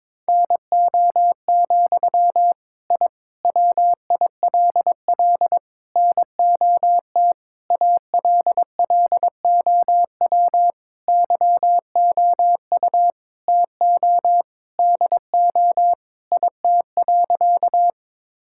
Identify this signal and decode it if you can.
Morse code, heard as a keyed tone, about 22 wpm, 700 Hz. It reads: NO, I WILL NOT ALLOW YOU TO DO IT.
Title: Sentences from Top 1000 Words - Review Part 5 - 22wpm